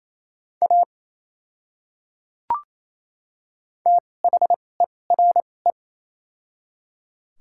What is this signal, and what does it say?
Morse code, heard as a keyed tone, about 28 wpm, spaced as for 20 wpm, 700 Hz. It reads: A THERE
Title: 2x Character Spacing - Top 100 Words - 28wpm